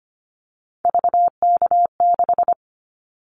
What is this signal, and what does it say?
Morse code, heard as a keyed tone, about 25 wpm, 700 Hz. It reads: VK6
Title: Callsign Prefixes for Top 100 DXCC - 25wpm